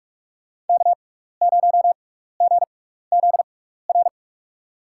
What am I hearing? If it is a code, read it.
Morse code, heard as a keyed tone, about 46 wpm, spaced as for 16 wpm, 700 Hz. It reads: K0GZR
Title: USA Call Signs - 6x Character Spacing - 45wpm